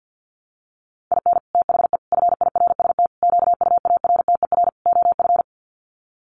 Morse code, heard as a keyed tone, about 50 wpm, 700 Hz. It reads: IN THE FIRST QUARTER OF
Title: Sets of 5 Words Encore2 50wpm